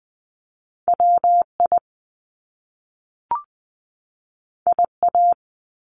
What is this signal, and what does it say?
Morse code, heard as a keyed tone, about 20 wpm, 700 Hz. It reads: WI IA
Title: US States 20wpm